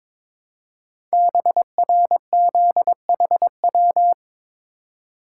Morse code, heard as a keyed tone, about 22 wpm, 700 Hz. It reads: BRZHW
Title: Code Groups - 5 Letters - 22wpm